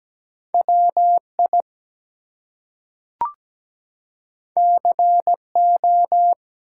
Morse code, heard as a keyed tone, about 17 wpm, 700 Hz. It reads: WI CO
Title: US States and Canadian Provinces 17wpm